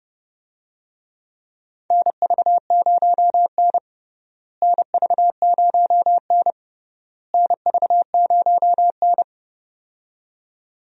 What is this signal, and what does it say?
Morse code, heard as a keyed tone, about 30 wpm, 700 Hz. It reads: NV0N NV0N NV0N
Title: USA Call Signs - 4 Characters - Repeated 3x - 30wpm